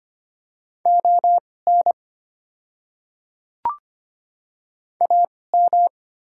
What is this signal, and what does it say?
Morse code, heard as a keyed tone, about 25 wpm, spaced as for 18 wpm, 690 Hz. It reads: ON AM
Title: 2x Character Spacing - Top 500 Words - 25wpm